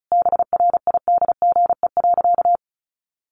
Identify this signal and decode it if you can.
Morse code, heard as a keyed tone, about 34 wpm, 700 Hz. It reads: BRIDGE.
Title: Sentences from Top 1500 Words 35wpm